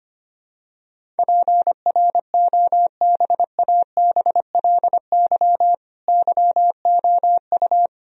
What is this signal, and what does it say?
Morse code, heard as a keyed tone, about 25 wpm, 700 Hz. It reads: PROBABLY YOU
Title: Sentences from Top 1000 Words - Review Part 4 - 25wpm